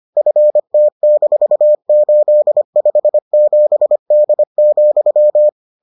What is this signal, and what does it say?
Morse code, heard as a keyed tone, about 25 wpm, 600 Hz. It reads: FT-857D,